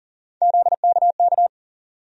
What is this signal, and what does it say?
Morse code, heard as a keyed tone, about 40 wpm, 700 Hz. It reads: ZKK
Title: USA Call Sign Postfixes - 3 Letters 40wpm